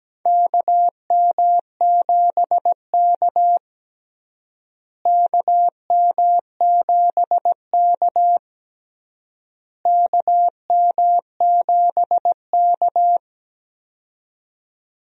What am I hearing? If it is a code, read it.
Morse code, heard as a keyed tone, about 17 wpm, 700 Hz. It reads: KM7K KM7K KM7K
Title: USA Call Signs - 4 Characters - Repeated 3x - 17wpm